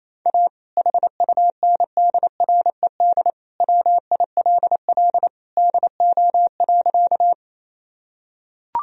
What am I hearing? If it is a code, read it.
Morse code, heard as a keyed tone, about 28 wpm, 700 Hz. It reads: A HUNDRED WILL DO.
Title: Sentences from Top 1000 Words - Review Part 2 - 28wpm